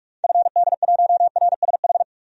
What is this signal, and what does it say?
Morse code, heard as a keyed tone, about 41 wpm, 700 Hz. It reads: FD1RSH
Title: DXCC-100 Call Signs - 6 Characters - 45wpm